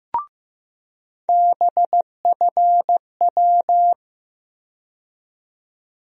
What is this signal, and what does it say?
Morse code, heard as a keyed tone, about 15 wpm, 700 Hz. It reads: BFW
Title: USA Call Sign Postfixes - 3 Letters 15wpm